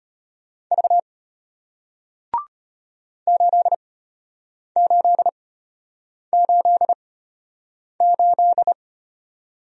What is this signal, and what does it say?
Morse code, heard as a keyed tone, about 33 wpm, spaced as for 8 wpm, 700 Hz. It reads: V 8888